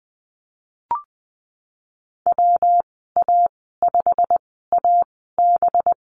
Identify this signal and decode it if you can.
Morse code, heard as a keyed tone, about 20 wpm, spaced as for 15 wpm, 700 Hz. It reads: WA5AB
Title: 219 Lesson 35g - Number 0 - Call Signs 20wpm